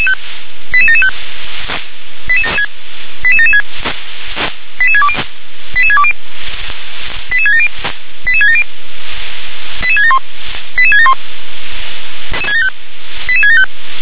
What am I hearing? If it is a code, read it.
Five-tone selective call (ZVEI), heard as quick runs of five tones, tones of about 70 ms. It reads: 88775 88776 88620 88620 88670 88670 88671 88671 88675